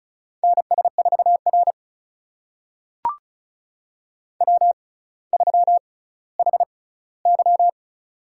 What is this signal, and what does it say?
Morse code, heard as a keyed tone, about 36 wpm, spaced as for 18 wpm, 700 Hz. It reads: NS4R W3HY
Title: USA Call Signs - 4 Characters - 6x Character Spacing - 35wpm